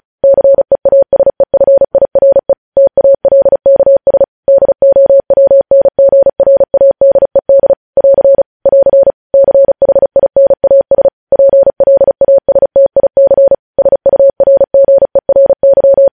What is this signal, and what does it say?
Morse code, heard as a keyed tone, about 36 wpm, 550 Hz. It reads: CEASEFIRE TALKS DOWNGRADED <AR> <AR> CHINAS PLASTIC SURGERY
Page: News Headlines in Morse Code at 35 WPM